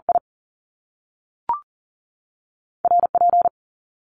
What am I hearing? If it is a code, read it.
Morse code, heard as a keyed tone, about 41 wpm, spaced as for 6 wpm, 700 Hz. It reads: IRP
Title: Top Two Letter Combinations in Words 40wpm